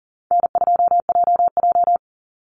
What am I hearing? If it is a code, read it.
Morse code, heard as a keyed tone, about 40 wpm, 700 Hz. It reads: N2JJ
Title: Call Signs Repeated 3x - 40wpm